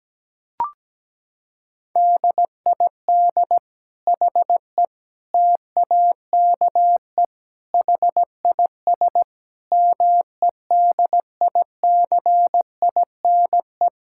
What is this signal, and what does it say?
Morse code, heard as a keyed tone, about 17 wpm, 700 Hz. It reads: DID HE TAKE HIS MEDICINE
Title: Sentences from Top 1000 Words - Review Part 6 - 17wpm